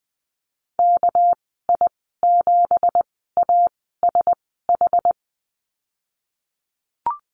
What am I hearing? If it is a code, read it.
Morse code, heard as a keyed tone, about 20 wpm, spaced as for 14 wpm, 700 Hz. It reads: KI7ASH
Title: 200 Lesson 32g - Letter k - Call Signs 20wpm